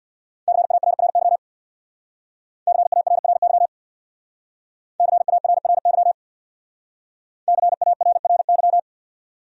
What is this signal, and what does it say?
Morse code, heard as a keyed tone, about 64 wpm, 700 Hz. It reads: CARRY CARRY CARRY CARRY